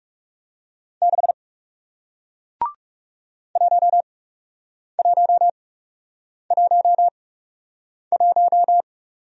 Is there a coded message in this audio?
Yes: Morse code from 1.0 to 8.8 s, heard as a keyed tone, about 40 wpm, spaced as for 8 wpm, 700 Hz. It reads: 6 1111